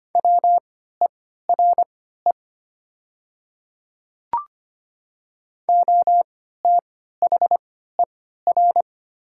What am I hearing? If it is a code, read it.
Morse code, heard as a keyed tone, about 25 wpm, spaced as for 14 wpm, 700 Hz. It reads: WERE OTHER
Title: 3x Character Spacing - Top 100 Words - 25wpm